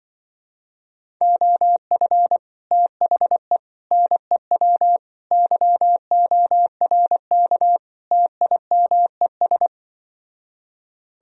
Morse code, heard as a keyed tone, about 24 wpm, 700 Hz. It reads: OF THE NEW YORK TIMES
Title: Speed-Racing - Sets of 5 Words 20wpm